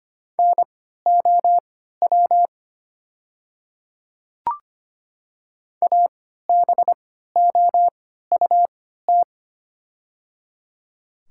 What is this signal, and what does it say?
Morse code, heard as a keyed tone, about 25 wpm, spaced as for 14 wpm, 700 Hz. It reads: NOW ABOUT
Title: 3x Character Spacing - Top 100 Words - 25wpm